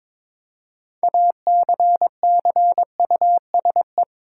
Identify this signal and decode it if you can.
Morse code, heard as a keyed tone, about 22 wpm, 700 Hz. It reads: ACCUSE
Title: Six Letter Words - 22wpm